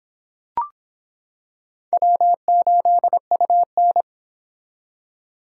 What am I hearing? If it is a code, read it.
Morse code, heard as a keyed tone, about 25 wpm, 700 Hz. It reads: W8UN